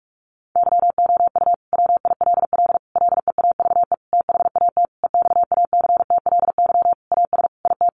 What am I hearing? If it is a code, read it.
Morse code, heard as a keyed tone, about 44 wpm, 700 Hz. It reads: YOU WILL LEAVE THAT EXACTLY AS IT
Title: Sentences from Top 900 Words 45wpm